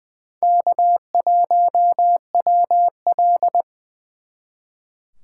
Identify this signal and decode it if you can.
Morse code, heard as a keyed tone, about 24 wpm, 700 Hz. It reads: K1WL
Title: Speed Racing - USA Call Signs 20wpm